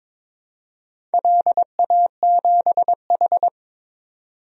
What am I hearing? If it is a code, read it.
Morse code, heard as a keyed tone, about 22 wpm, 700 Hz. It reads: LA7H
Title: DXCC-100 Call Signs - 4 Characters - 22wpm